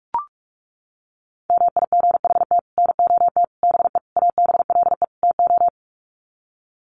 Morse code, heard as a keyed tone, about 44 wpm, 700 Hz. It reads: MIGHT NOT BE ABLE TO
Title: Sets of 5 Words Encore2 45wpm